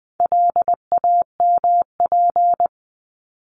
Morse code, heard as a keyed tone, about 20 wpm, 700 Hz. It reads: LAMP
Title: 260 Lesson 42e - Words 20-20wpm